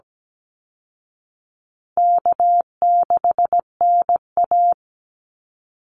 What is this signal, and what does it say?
Morse code, heard as a keyed tone, about 17 wpm, 700 Hz. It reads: K6NA
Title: Easy Call Signs Repeated 3x - 17wpm